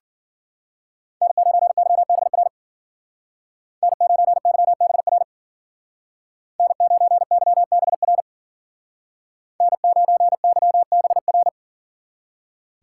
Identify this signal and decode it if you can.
Morse code, heard as a keyed tone, about 52 wpm, 700 Hz. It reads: N9YBR N9YBR N9YBR N9YBR